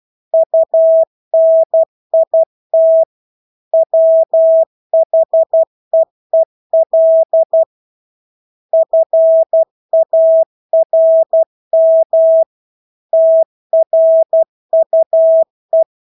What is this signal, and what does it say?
Morse code, heard as a keyed tone, about 12 wpm, 650 Hz. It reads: UNIT WHEEL FARM TRUE
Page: Engelske ord 12 wpm